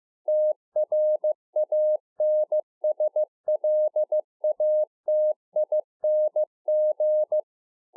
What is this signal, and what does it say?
Morse code, heard as a keyed tone, about 15 wpm, 610 Hz. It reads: TRANSLATING